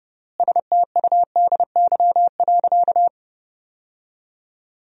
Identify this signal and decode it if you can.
Morse code, heard as a keyed tone, about 30 wpm, 700 Hz. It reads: STUDY.
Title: 4x Word-Spacing - Sentences from Top 300 Words - 30wpm